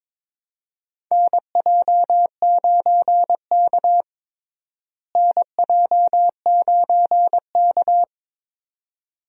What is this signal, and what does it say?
Morse code, heard as a keyed tone, about 21 wpm, 700 Hz. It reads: NJ9K NJ9K
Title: USA Call Signs - 4 Characters - Repeated 2x - 22wpm